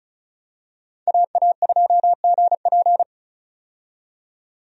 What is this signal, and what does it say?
Morse code, heard as a keyed tone, about 34 wpm, 700 Hz. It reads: AA2GP
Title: USA Call Signs - 5 Characters - 35wpm